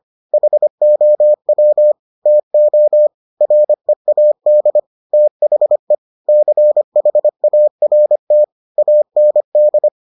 Morse code, heard as a keyed tone, about 25 wpm, 600 Hz. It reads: HOW TO READ THE CHART AND